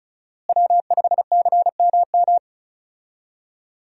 Morse code, heard as a keyed tone, about 34 wpm, 700 Hz. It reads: W5CMM
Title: USA Call Signs - 5 Characters - Repeated 2x - 35wpm